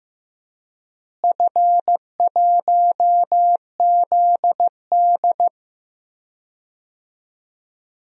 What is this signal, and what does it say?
Morse code, heard as a keyed tone, about 16 wpm, 700 Hz. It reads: F1ZD
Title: DXCC-100 Call Signs - 4 Characters - Repeated 2x - 15wpm